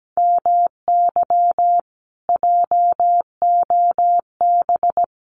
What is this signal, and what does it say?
Morse code, heard as a keyed tone, about 17 wpm, 700 Hz. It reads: MY JOB
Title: Common QSO Phrases Encore 17wpm